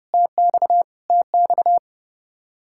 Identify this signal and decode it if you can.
Morse code, heard as a keyed tone, about 30 wpm, 700 Hz. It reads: TX TX
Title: Practice QSO - Exchange 1 - Beginning - 30wpm